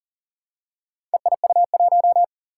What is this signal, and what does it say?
Morse code, heard as a keyed tone, about 40 wpm, 700 Hz. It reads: EIU1
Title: Four Challenging Letter-Number Combinations 40wpm